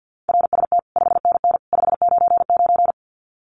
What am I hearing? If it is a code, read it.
Morse code, heard as a keyed tone, about 48 wpm, 700 Hz. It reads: RST 5NN 599
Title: Common QSO Phrases 50wpm